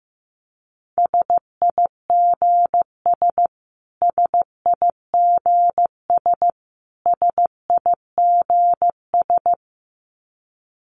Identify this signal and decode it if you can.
Morse code, heard as a keyed tone, about 15 wpm, 700 Hz. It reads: SIGS SIGS SIGS
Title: Abbreviations Repeated 3x - 15wpm